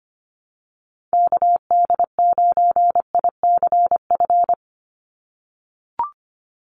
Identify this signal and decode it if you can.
Morse code, heard as a keyed tone, about 25 wpm, 690 Hz. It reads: KD9ICF
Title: USA Call Signs 25wpm